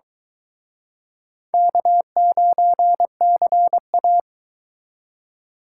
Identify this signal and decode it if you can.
Morse code, heard as a keyed tone, about 24 wpm, 700 Hz. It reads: K9CA